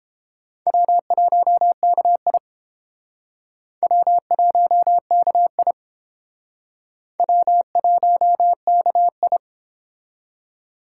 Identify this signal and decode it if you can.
Morse code, heard as a keyed tone, about 30 wpm, 700 Hz. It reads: W1KI W1KI W1KI